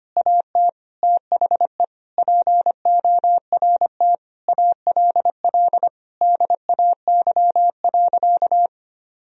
Morse code, heard as a keyed tone, about 25 wpm, 700 Hz. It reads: AT THE PORT ALL DAY.
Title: Sentences from Top 1000 Words - Review Part 1 - 25wpm